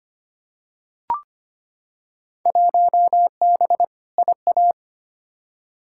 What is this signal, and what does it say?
Morse code, heard as a keyed tone, about 25 wpm, 700 Hz. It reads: 1B IA
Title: Field Day Exchanges Encore - 25wpm